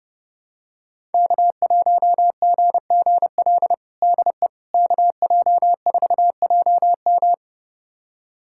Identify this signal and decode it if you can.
Morse code, heard as a keyed tone, about 32 wpm, 700 Hz. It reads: K1GGL DE KJ4JM
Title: USA Callsign DE Callsign - 5 Characters - 30wpm